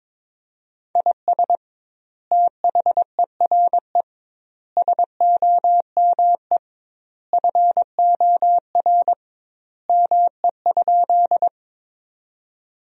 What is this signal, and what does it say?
Morse code, heard as a keyed tone, about 21 wpm, 700 Hz. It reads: IS THERE SOME FOR ME?
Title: 2x Word-Spacing - Sentences from Top 200 Words - 22wpm